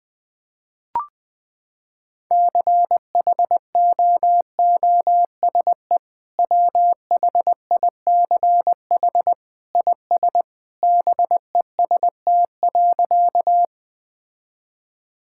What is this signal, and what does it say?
Morse code, heard as a keyed tone, about 21 wpm, 700 Hz. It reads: CHOOSE WHICH IS BEST.
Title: Sentences with 2-Word Phrases - 20wpm